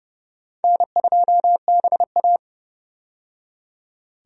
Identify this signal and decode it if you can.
Morse code, heard as a keyed tone, about 29 wpm, 700 Hz. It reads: N2BA
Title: USA Call Signs - 4 Characters - 30wpm